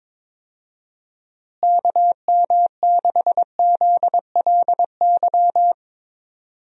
Morse code, heard as a keyed tone, about 22 wpm, 700 Hz. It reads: KM6ZLY
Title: USA Call Signs - 6 Characters - Repeated 2x - 22wpm